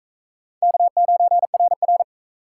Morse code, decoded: K9RR

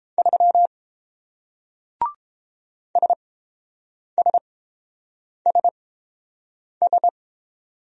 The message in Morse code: 3 SSSS

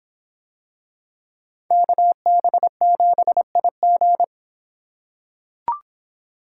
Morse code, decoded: KB7IG

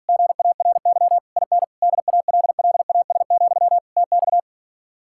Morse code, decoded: GARY IN DALLAS, TX

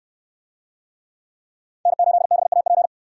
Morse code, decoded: N8BAP

